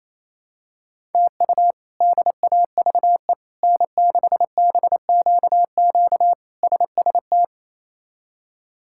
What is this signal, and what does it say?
Morse code, heard as a keyed tone, about 28 wpm, 700 Hz. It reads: TU DAVE N6BQQ SST
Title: K1USN SST QSO Practice - 28wpm